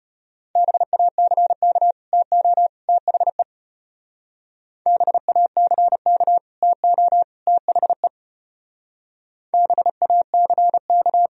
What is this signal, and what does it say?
Morse code, heard as a keyed tone, about 31 wpm, 700 Hz. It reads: BACK TO THE BACK TO THE BACK
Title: Speed-Racing - Sets of 3 Words 25wpm